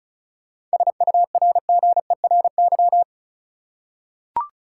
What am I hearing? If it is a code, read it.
Morse code, heard as a keyed tone, about 34 wpm, 700 Hz. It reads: SURGERY
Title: Seven Letter Words - 35wpm